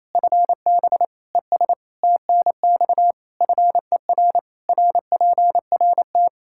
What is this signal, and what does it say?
Morse code, heard as a keyed tone, about 28 wpm, 700 Hz. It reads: FB ES TNX FER RPRT
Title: K1USN SST QSO Practice 28wpm